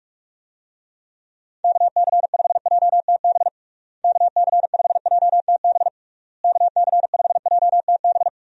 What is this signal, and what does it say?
Morse code, heard as a keyed tone, about 46 wpm, 700 Hz. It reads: KC5JTB KC5JTB KC5JTB
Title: USA Call Signs - 6 Characters - Repeated 3x - 45wpm